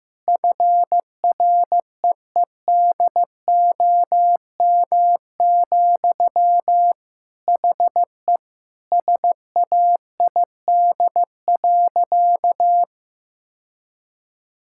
Morse code, heard as a keyed tone, about 15 wpm, 700 Hz. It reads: FREEDOM, HE SAID.
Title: Sentences from Top 1000 Words - Review Part 5 - 15wpm